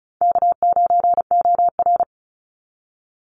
Morse code, heard as a keyed tone, about 35 wpm, 700 Hz. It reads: K9OR
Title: Call Signs Repeated 2x - 35wpm